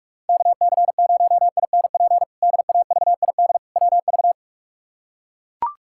Morse code, heard as a keyed tone, about 42 wpm, 700 Hz. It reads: KC0INP DAVID WV
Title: K1USN SST QSO Practice - 45wpm